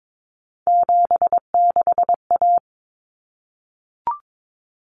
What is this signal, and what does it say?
Morse code, heard as a keyed tone, about 22 wpm, 700 Hz. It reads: 76A
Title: Common Cut Numbers - 3-Digit 22wpm